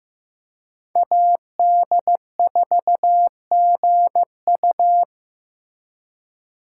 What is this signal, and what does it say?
Morse code, heard as a keyed tone, about 18 wpm, 700 Hz. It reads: AD4GU
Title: Speed Racing - USA 2x2 Call Signs 15wpm